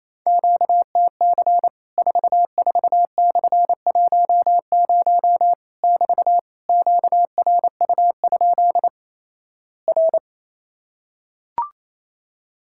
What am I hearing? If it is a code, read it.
Morse code, heard as a keyed tone, about 28 wpm, 700 Hz. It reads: QTC 44/10 <BT> QRU? R
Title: Worked All Europe DX Contest - QTC Exchange - 28wpm